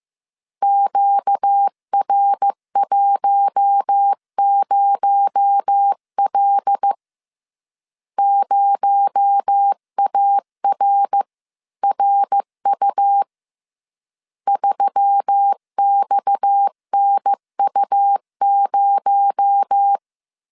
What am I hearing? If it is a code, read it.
Morse code, heard as a keyed tone, about 15 wpm, 800 Hz. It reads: QR10L 0ARRU 3XNU0